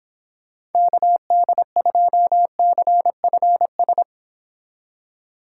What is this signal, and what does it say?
Morse code, heard as a keyed tone, about 29 wpm, 700 Hz. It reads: KD2CFS